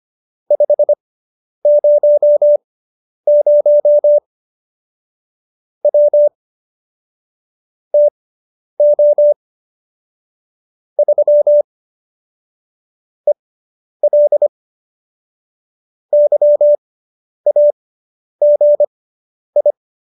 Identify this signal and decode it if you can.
Morse code, heard as a keyed tone, about 25 wpm, spaced as for 10 wpm, 590 Hz. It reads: 500 W TO 3 EL YAGI